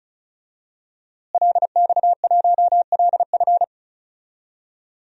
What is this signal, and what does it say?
Morse code, heard as a keyed tone, about 36 wpm, 700 Hz. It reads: LX1LF
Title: DXCC-100 Call Signs - 5 Characters - Repeated 3x - 35wpm